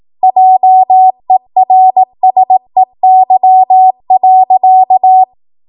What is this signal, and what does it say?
Morse code, heard as a keyed tone, about 18 wpm, 750 Hz. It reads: JERSEY.
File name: Archive / 18 WPM / 18WPM